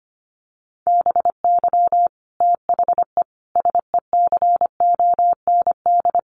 Morse code, heard as a keyed tone, about 25 wpm, 700 Hz. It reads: BY THE SECOND
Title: Sentences from Top 1200 Words 25wpm